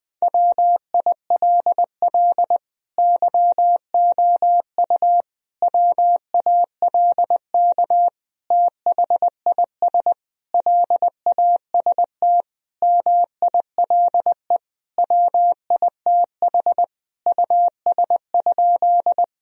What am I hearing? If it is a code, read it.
Morse code, heard as a keyed tone, about 21 wpm, 700 Hz. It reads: WILL YOU WALK THIS LAST MILE WITH US?
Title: Sentences from Top 1000 Words - Review Part 1 - 20wpm